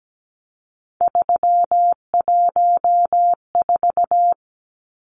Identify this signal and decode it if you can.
Morse code, heard as a keyed tone, about 17 wpm, 700 Hz. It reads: 314
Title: Numbers - 3-Digit 17wpm